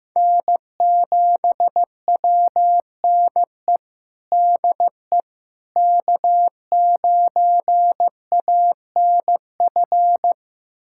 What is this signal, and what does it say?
Morse code, heard as a keyed tone, about 15 wpm, 700 Hz. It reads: N7WNE DE K9ANF
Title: USA Callsign DE Callsign - 5 Characters - 15wpm